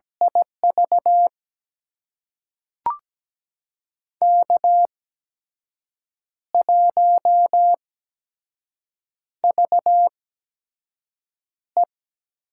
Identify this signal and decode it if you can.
Morse code, heard as a keyed tone, about 17 wpm, spaced as for 6 wpm, 700 Hz. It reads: IV K1VE
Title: USA Call Signs - 8x Character Spacing - 17wpm